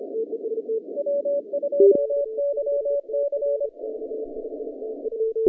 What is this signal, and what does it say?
Morse code, heard as a keyed tone, about 25 wpm, 570 Hz. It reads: W2YC
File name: 2015 cq ww cw / 2015 cqwww k5tr - 0-7